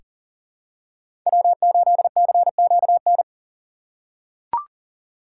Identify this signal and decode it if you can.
Morse code, heard as a keyed tone, about 40 wpm, 700 Hz. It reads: W8CQN
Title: USA Call Signs - 5 Characters - 4x Character Spacing - 40wpm